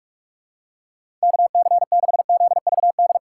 Morse code, heard as a keyed tone, about 46 wpm, 700 Hz. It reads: KC6ZVD
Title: USA Call Signs - 6 Characters - 3x Character Spacing - 45wpm